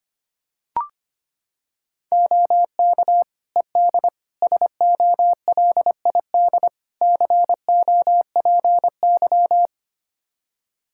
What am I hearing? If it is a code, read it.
Morse code, heard as a keyed tone, about 25 wpm, 700 Hz. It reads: OK ED SOLID COPY
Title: Practice QSO - Exchange 3 - More Info - 25wpm